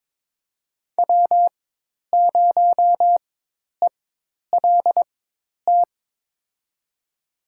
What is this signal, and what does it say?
Morse code, heard as a keyed tone, about 22 wpm, spaced as for 10 wpm, 700 Hz. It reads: W0ELT
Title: USA Call Signs - 4x Character Spacing - 22wpm